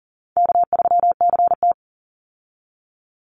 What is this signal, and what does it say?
Morse code, heard as a keyed tone, about 40 wpm, 700 Hz. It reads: K3CT
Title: Easy Call Signs Repeated 3x - 40wpm